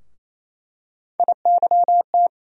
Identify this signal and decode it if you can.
Morse code, heard as a keyed tone, about 28 wpm, 700 Hz. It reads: IYT
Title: USA Call Sign Postfixes - 3 Letters 28wpm